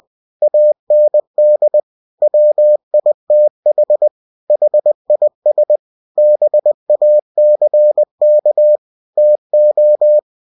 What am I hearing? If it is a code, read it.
Morse code, heard as a keyed tone, about 20 wpm, 600 Hz. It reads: AND WITH HIS BACK TO
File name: winnie-20wpm